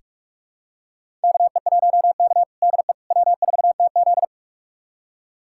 Morse code, heard as a keyed tone, about 43 wpm, 700 Hz. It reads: KE1K DE W4TZ
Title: USA Callsign DE Callsign - 4 Characters - 45wpm